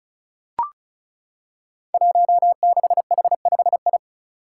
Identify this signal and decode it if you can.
Morse code, heard as a keyed tone, about 34 wpm, 700 Hz. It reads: 16H5I
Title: Five Challenging Letter-Number Combinations 35wpm